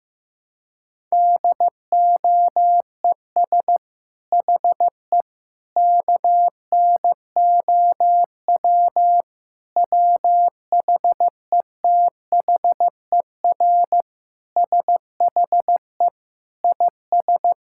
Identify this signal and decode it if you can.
Morse code, heard as a keyed tone, about 15 wpm, 700 Hz. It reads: DOES HE KNOW WHETHER SHE IS
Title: Sentences from Top 1000 Words - Review Part 6 - 15wpm